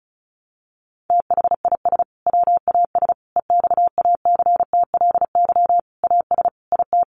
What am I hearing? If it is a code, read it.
Morse code, heard as a keyed tone, about 34 wpm, 700 Hz. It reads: THIS WAS EXACTLY AS IT
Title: Sentences from Top 900 Words 35wpm